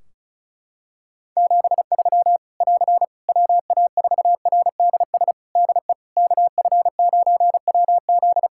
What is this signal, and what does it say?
Morse code, heard as a keyed tone, about 36 wpm, 700 Hz. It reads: 73 <AR> WA4RDS DE KF9WZ
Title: K1USN SST QSO Practice 35wpm